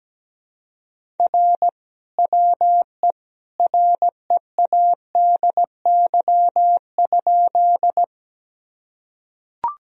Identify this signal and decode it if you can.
Morse code, heard as a keyed tone, about 17 wpm, 700 Hz. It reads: R WE READY?
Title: Sentences from Top 400 Words and Abbreviations 17wpm